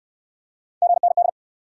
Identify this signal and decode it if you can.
Morse code, heard as a keyed tone, about 67 wpm, 700 Hz. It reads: GND